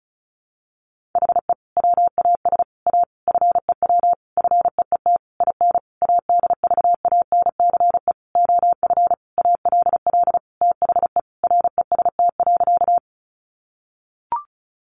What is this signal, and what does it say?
Morse code, heard as a keyed tone, about 34 wpm, 700 Hz. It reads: HE WAS A FEW FEET IN ADVANCE OF ALL THE REST.
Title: Sentences from Top 1400 Words 35wpm